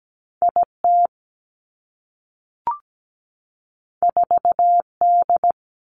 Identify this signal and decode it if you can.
Morse code, heard as a keyed tone, about 17 wpm, 700 Hz. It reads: IT 4D
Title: Two Challenging Letter-Number Combinations 17wpm